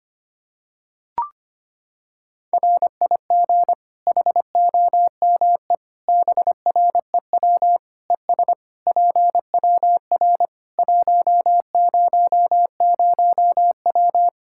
Morse code, heard as a keyed tone, about 25 wpm, 700 Hz. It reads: RIG HOME BREW ES PWR 100W
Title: Practice QSO - Exchange 2 - Info - 25wpm